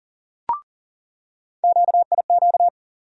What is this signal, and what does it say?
Morse code, heard as a keyed tone, about 39 wpm, 700 Hz. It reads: QIQ
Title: USA Call Sign Postfixes - 3 Letters 40wpm